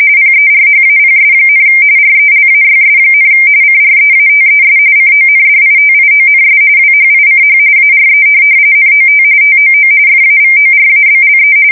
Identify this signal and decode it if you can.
RTTY, about 45.45 baud. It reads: PERHAPS, RELAYS, CREATES CONDITIONS WHICH ARE NOT TOO FAVORABLE
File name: fsk170 / morkrum